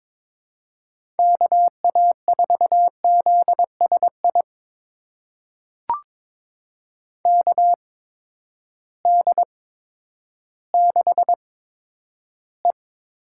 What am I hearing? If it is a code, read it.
Morse code, heard as a keyed tone, about 23 wpm, spaced as for 11 wpm, 700 Hz. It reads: KA4ZSI KD6E
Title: USA Call Signs - 6 Characters - 8x Character Spacing - 22wpm